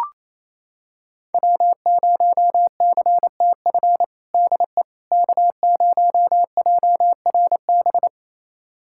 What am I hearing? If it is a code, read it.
Morse code, heard as a keyed tone, about 28 wpm, 700 Hz. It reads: W0CTF DE K0JRB
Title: USA Callsign DE Callsign - 5 Characters - 28wpm